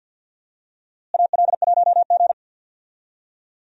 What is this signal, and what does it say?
Morse code, heard as a keyed tone, about 52 wpm, 700 Hz. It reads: AL1G
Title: USA Call Signs - 4 Characters - Repeated 2x - 50wpm